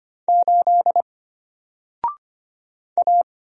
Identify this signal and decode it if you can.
Morse code, heard as a keyed tone, about 25 wpm, 700 Hz. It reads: 8 A
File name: Single-Letter-Number-25wpm